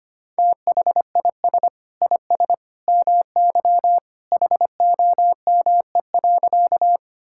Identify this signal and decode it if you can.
Morse code, heard as a keyed tone, about 25 wpm, 700 Hz. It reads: THIS IS MY HOME.
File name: practice / 25wpm / Sentences with 2-Word Phrases - 25wpm